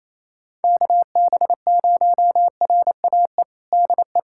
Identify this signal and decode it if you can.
Morse code, heard as a keyed tone, about 28 wpm, 700 Hz. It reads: KB0RAE DE
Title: USA Callsign DE Callsign 28wpm